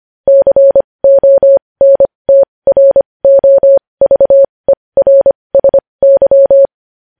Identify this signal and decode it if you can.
Morse code, heard as a keyed tone, about 25 wpm, spaced as for 20 wpm, 550 Hz. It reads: CONTROVERSY
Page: news-headlines-in-morse-code-at-20-wpm